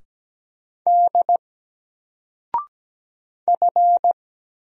Morse code, heard as a keyed tone, about 17 wpm, 700 Hz. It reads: D F
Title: Single Letter-Number Alt 17wpm